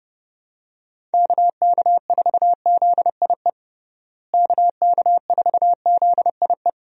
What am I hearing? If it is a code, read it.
Morse code, heard as a keyed tone, about 32 wpm, 700 Hz. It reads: KK4ZIE KK4ZIE
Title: USA Call Signs - 6 Characters - Repeated 3x - 30wpm